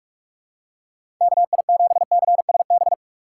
Morse code, heard as a keyed tone, about 42 wpm, 700 Hz. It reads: KI7CSB
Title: USA Call Signs - 3x Character Spacing - 45wpm